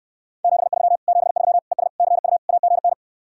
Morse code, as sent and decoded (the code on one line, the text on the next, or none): --... ...-- / --... ...-- / . ... / -.-. ..- / .- --. -.
73 73 ES CU AGN